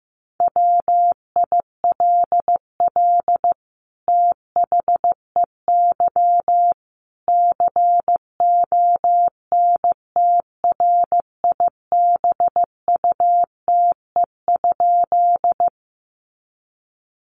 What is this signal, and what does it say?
Morse code, heard as a keyed tone, about 15 wpm, 700 Hz. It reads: WILL THEY CONTRIBUTE?
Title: Sentences from Top 1400 Words 15wpm